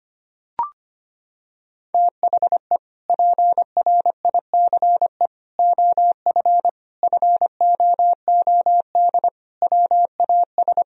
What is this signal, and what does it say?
Morse code, heard as a keyed tone, about 25 wpm, 700 Hz. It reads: THE PRICE OF FOOD WAS
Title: Sentences from Top 1000 Words - Review Part 4 - 25wpm